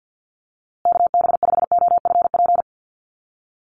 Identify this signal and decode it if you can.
Morse code, heard as a keyed tone, about 50 wpm, 700 Hz. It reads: KB5OFP
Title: USA Call Signs Repeated 2x 50wpm